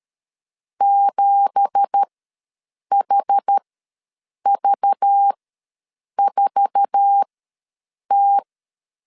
Morse code, heard as a keyed tone, about 13 wpm, spaced as for 7 wpm, 800 Hz. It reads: 7HV4T
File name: LEZIONI / 10 SIGNS / 10B INT ws 12wpm 3527